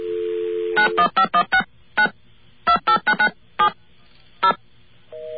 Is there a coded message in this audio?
Yes: DTMF from 0.8 to 4.5 s, heard as telephone touch tones, tones of about 90 ms. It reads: 91516 9 28997 0